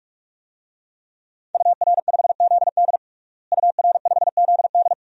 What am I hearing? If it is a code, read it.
Morse code, heard as a keyed tone, about 47 wpm, 700 Hz. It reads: UR5ZD UR5ZD
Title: DXCC-100 Call Signs - 5 Characters - Repeated 2x - 45wpm